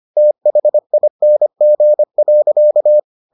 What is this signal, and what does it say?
Morse code, heard as a keyed tone, about 25 wpm, 600 Hz. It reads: THING.